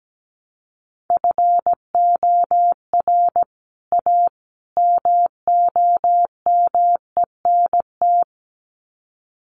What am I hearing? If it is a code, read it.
Morse code, heard as a keyed tone, about 17 wpm, 700 Hz. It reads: FOR A MOMENT
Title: Sets of 3 Words 17wpm